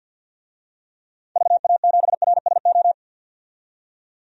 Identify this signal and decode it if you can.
Morse code, heard as a keyed tone, about 51 wpm, 700 Hz. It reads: VA7RSO
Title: DXCC-100 Call Signs - 6 Characters - 50wpm